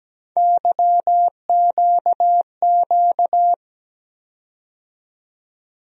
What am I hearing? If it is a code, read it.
Morse code, heard as a keyed tone, about 17 wpm, 700 Hz. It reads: YQQ
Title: USA Call Sign Postfixes - 3 Letters 17wpm